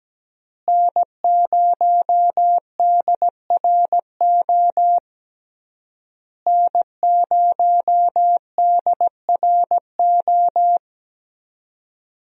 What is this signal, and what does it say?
Morse code, heard as a keyed tone, about 17 wpm, 700 Hz. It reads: N0DRO N0DRO
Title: USA Call Signs - 5 Characters - Repeated 3x - 17wpm